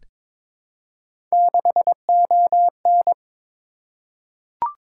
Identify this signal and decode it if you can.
Morse code, heard as a keyed tone, about 22 wpm, 700 Hz. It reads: 6ON
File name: Three Letter-Number Combinations - 22wpm